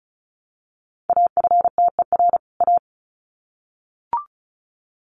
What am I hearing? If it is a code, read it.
Morse code, heard as a keyed tone, about 34 wpm, 700 Hz. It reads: AFTER A
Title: Sets of 2 Words 35wpm